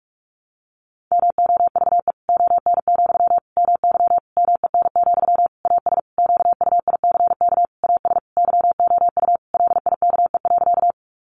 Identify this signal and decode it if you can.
Morse code, heard as a keyed tone, about 44 wpm, 700 Hz. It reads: MOVE ON, MY MEN, AS QUICK AS YOU LIKE.
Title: Sentences from Top 500 Words 45wpm